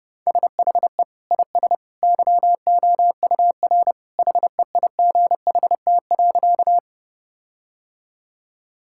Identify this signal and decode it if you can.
Morse code, heard as a keyed tone, about 30 wpm, 700 Hz. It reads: SHE IS YOUR HEIGHT.
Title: Sentences from Top 1600 Words 30wpm